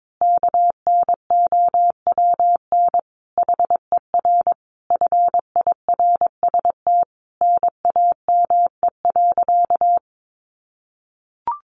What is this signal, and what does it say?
Morse code, heard as a keyed tone, about 22 wpm, 700 Hz. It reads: KNOWN HER FIRST NAME.
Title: Sentences from Top 1200 Words 22wpm